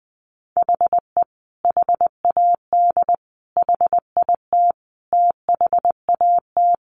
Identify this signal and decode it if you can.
Morse code, heard as a keyed tone, about 20 wpm, 700 Hz. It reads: HE HAD HIT THAT
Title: Sentences from Top 1200 Words 20wpm